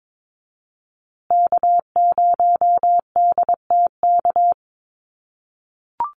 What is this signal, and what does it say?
Morse code, heard as a keyed tone, about 22 wpm, 700 Hz. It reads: K0DTK
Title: USA Call Signs Repeated 3x 22wpm